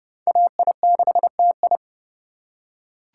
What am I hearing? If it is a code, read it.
Morse code, heard as a keyed tone, about 30 wpm, 700 Hz. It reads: AI6TI